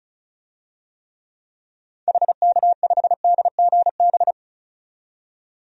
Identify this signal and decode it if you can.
Morse code, heard as a keyed tone, about 34 wpm, 700 Hz. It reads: HK5DGB
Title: DXCC-100 Call Signs - 6 Characters - 35wpm